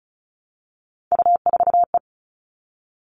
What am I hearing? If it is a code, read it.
Morse code, heard as a keyed tone, about 34 wpm, 700 Hz. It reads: U4E
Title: All Cut Numbers - 3-Digit 35wpm